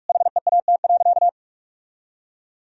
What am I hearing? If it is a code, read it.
Morse code, heard as a keyed tone, about 43 wpm, 700 Hz. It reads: HEAT.
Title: Sentences from Top 1000 Words - Review Part 3 - 45wpm